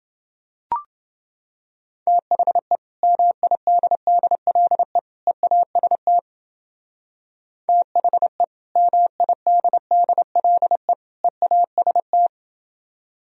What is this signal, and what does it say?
Morse code, heard as a keyed tone, about 31 wpm, 700 Hz. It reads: THE MIDDLE EAST THE MIDDLE EAST